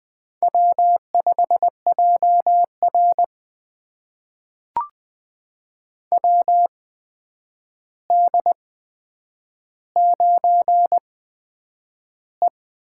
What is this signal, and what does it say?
Morse code, heard as a keyed tone, about 20 wpm, spaced as for 9 wpm, 700 Hz. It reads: W5JR WD9E
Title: USA Call Signs - 4 Characters - 8x Character Spacing - 20wpm